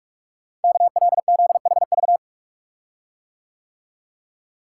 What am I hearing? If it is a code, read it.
Morse code, heard as a keyed tone, about 44 wpm, 700 Hz. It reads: KLZHV
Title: Code Groups - 5 Letters - 45wpm